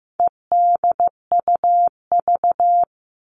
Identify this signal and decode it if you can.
Morse code, heard as a keyed tone, about 15 wpm, 700 Hz. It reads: EDUV
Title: All Cut Numbers - 4-Digit 15wpm